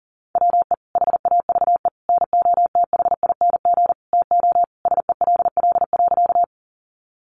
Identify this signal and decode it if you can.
Morse code, heard as a keyed tone, about 40 wpm, 700 Hz. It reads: WE HAVE NOTHING TO SELL.
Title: Sentences from Top 1100 Words 40wpm